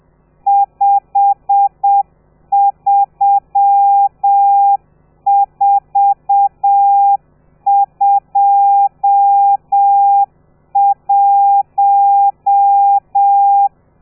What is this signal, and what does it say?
Morse code, heard as a keyed tone, about 7 wpm, 800 Hz. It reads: 53421
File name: telegrafia / i2tfj / Lez12 30